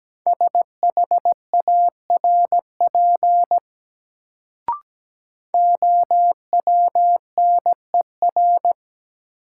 Morse code, heard as a keyed tone, about 17 wpm, 700 Hz. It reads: SHARP OWNER
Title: Sentences from Top 1600 Words - New Word Drill - 17wpm